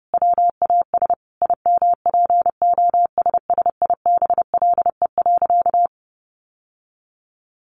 Morse code, heard as a keyed tone, about 30 wpm, 700 Hz. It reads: WAS IMPOSSIBLE.
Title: Sentences from Top 1100 Words 30wpm